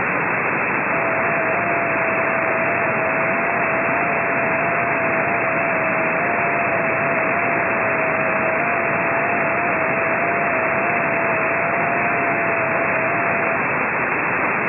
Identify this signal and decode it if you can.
FT8.